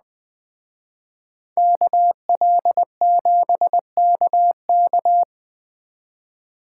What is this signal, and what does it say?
Morse code, heard as a keyed tone, about 21 wpm, 700 Hz. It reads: KL7KK